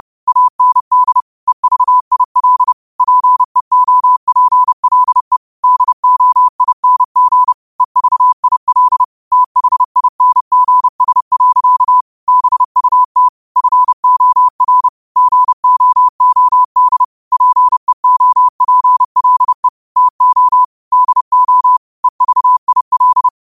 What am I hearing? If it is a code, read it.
Morse code, heard as a keyed tone, about 30 wpm, 1000 Hz. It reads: AND EVIL PEOPLE DOING EVIL THINGS. BUT FOR GOOD PEOPLE TO DO EVIL
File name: qotdfiles / 30wpm